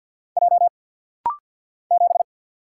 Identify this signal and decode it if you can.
Morse code, heard as a keyed tone, about 49 wpm, 700 Hz. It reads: J 7
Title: Single Letter-Number Alt - Rapid-Fire 50wpm